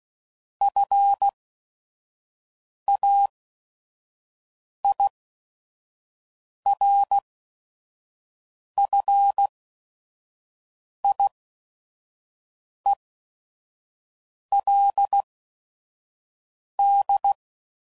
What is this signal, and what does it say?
Morse code, heard as a keyed tone, about 16 wpm, spaced as for 5 wpm, 800 Hz. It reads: FAIRFIELD
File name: CW Code Groups @5WPM / fromSM's / Cities